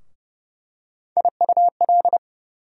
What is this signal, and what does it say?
Morse code, heard as a keyed tone, about 32 wpm, 700 Hz. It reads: IUL
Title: USA Call Sign Postfixes - 3 Letters 30wpm